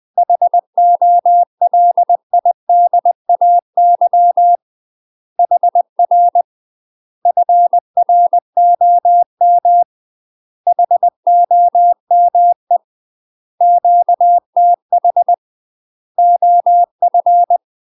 Morse code, heard as a keyed tone, about 20 wpm, 700 Hz. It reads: HOLIDAY HR FROM HOME QTH OF
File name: UK MASTER QSOs TRAINING / 20WPM / UK MASTER QSOs 0072